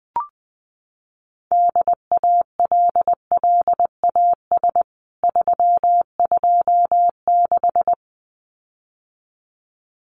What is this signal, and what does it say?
Morse code, heard as a keyed tone, about 20 wpm, 700 Hz. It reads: DALLAS 326
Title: CWT Exchange 20wpm